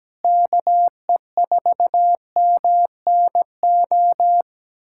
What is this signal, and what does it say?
Morse code, heard as a keyed tone, about 17 wpm, 700 Hz. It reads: KE4MNO
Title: USA Callsign DE Callsign 17wpm